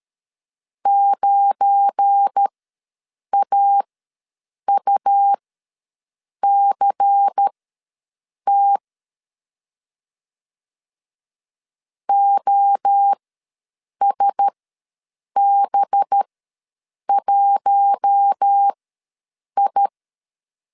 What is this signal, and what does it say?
Morse code, heard as a keyed tone, about 13 wpm, spaced as for 7 wpm, 800 Hz. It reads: 9AUCT OSB1I
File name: LEZIONI / 7 GF9Z / 7 GF9 ws 12wpm 3344